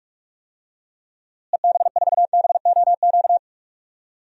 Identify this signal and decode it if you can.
Morse code, heard as a keyed tone, about 44 wpm, 700 Hz. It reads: EB4BOQ
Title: DXCC-100 Call Signs - 6 Characters - Repeated 3x - 45wpm